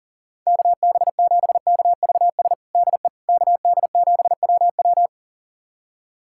Morse code, heard as a keyed tone, about 40 wpm, 700 Hz. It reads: KB7KVS DE KD7WW
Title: USA Callsign DE Callsign 40wpm